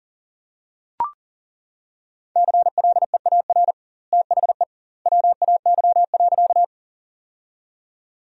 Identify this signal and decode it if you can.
Morse code, heard as a keyed tone, about 40 wpm, 700 Hz. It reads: CLEAR THE WAY.
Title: 2x Word-Spacing - Sentences from Top 500 Words - 40wpm